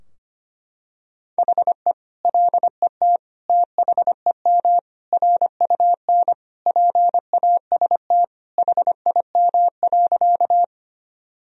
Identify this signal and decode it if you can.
Morse code, heard as a keyed tone, about 25 wpm, 700 Hz. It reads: HE LET THEM RUN PAST HIM.
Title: Sentences from Top 1000 Words - Review Part 3 - 25wpm